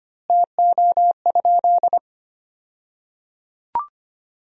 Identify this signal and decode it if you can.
Morse code, heard as a keyed tone, about 25 wpm, 700 Hz. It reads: TO?
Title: 2x Word-Spacing - Sentences from Top 100 Words - 25wpm